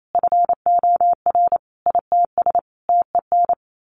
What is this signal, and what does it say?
Morse code, heard as a keyed tone, about 28 wpm, 700 Hz. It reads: FOR ITS TEN
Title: A Sound Of Thunder 28wpm